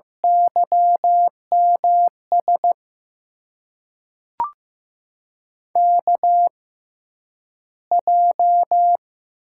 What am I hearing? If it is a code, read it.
Morse code, heard as a keyed tone, about 15 wpm, spaced as for 9 wpm, 700 Hz. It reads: YMS KJ
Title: USA Call Signs - 6 Characters - 6x Character Spacing - 15wpm